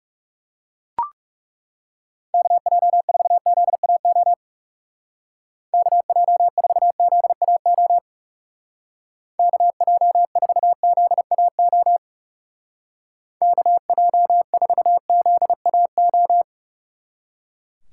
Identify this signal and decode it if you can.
Morse code, heard as a keyed tone, about 36 wpm, 700 Hz. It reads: KJ4ZAO KJ4ZAO KJ4ZAO KJ4ZAO
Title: Speed Racing - USA 2x3 Call Signs 30wpm